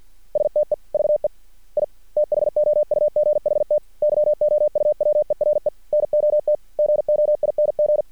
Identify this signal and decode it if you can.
Morse code, heard as a keyed tone, about 50 wpm, 600 Hz. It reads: STEVE I THOUGHT YOU WERE NOT GOING
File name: archived / CRSnet